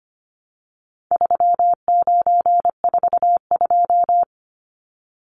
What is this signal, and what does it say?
Morse code, heard as a keyed tone, about 25 wpm, 700 Hz. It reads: 3942
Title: Numbers - 4-Digit 25wpm